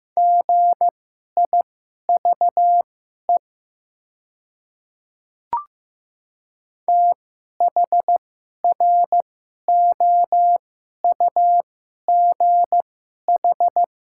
Morse code, heard as a keyed tone, about 15 wpm, spaced as for 11 wpm, 700 Hz. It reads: GIVE THROUGH
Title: 2x Character Spacing - Top 200 Words - 15wpm